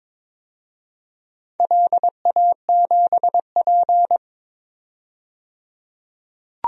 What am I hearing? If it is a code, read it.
Morse code, heard as a keyed tone, about 22 wpm, 700 Hz. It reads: LA7P
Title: DXCC-100 Call Signs - 4 Characters - 22wpm